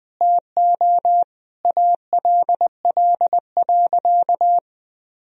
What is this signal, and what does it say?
Morse code, heard as a keyed tone, about 20 wpm, 700 Hz. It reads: TO ALL.
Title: Sentences from Top 1000 Words - Review Part 1 - 20wpm